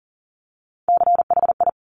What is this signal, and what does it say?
Morse code, heard as a keyed tone, about 41 wpm, 690 Hz. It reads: CHI
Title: Top Three Letter Combinations in Words 40wpm